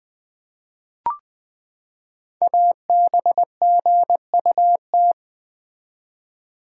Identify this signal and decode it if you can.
Morse code, heard as a keyed tone, about 20 wpm, 700 Hz. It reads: ABGUT
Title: Code Groups - 5 Letters - 20wpm